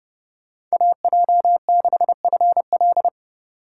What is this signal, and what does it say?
Morse code, heard as a keyed tone, about 36 wpm, 700 Hz. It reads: AJ6FL
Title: Speed Racing - USA 2x2 Call Signs 30wpm